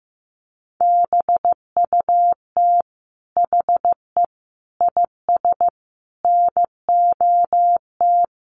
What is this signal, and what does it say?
Morse code, heard as a keyed tone, about 15 wpm, 700 Hz. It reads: BUT HE IS NOT
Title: Sentences from Top 1000 Words 15wpm